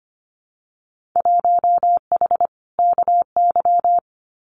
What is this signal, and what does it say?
Morse code, heard as a keyed tone, about 25 wpm, 700 Hz. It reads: 1H KY
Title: Winter Field Day Exchanges 25wpm